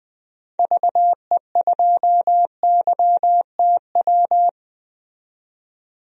Morse code, heard as a keyed tone, about 20 wpm, 700 Hz. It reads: VE2YTW
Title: DXCC-100 Call Signs - 6 Characters - 20wpm